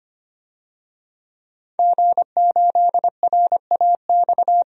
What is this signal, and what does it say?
Morse code, heard as a keyed tone, about 25 wpm, 700 Hz. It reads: G8RAX
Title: DXCC-100 Call Signs - 5 Characters - 25wpm